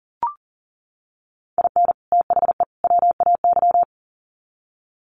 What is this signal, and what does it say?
Morse code, heard as a keyed tone, about 40 wpm, 700 Hz. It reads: IN THE WAY
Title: Sets of 3 Words 40wpm